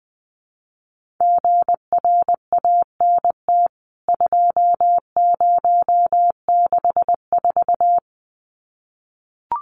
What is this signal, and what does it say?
Morse code, heard as a keyed tone, about 20 wpm, 700 Hz. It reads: GRANT 2064
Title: CWT Exchange 20wpm